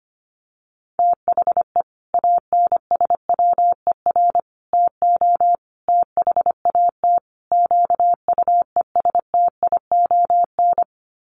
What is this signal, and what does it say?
Morse code, heard as a keyed tone, about 25 wpm, 700 Hz. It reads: THE ANSWER TO THAT QUESTION
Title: Sets of 5 Words Encore2 25wpm